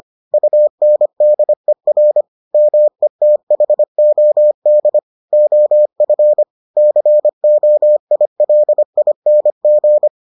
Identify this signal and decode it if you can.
Morse code, heard as a keyed tone, about 25 wpm, 600 Hz. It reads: UNDER METHOD OF COILING